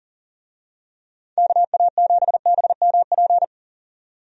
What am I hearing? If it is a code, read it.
Morse code, heard as a keyed tone, about 40 wpm, 700 Hz. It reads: KA7BMP
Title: USA Call Signs - 6 Characters - 40wpm